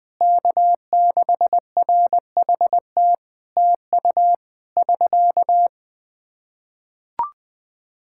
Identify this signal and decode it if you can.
Morse code, heard as a keyed tone, about 20 wpm, 700 Hz. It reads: K6RHT TU <SK>
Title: Practice QSO - Exchange 4 - Ending - 20wpm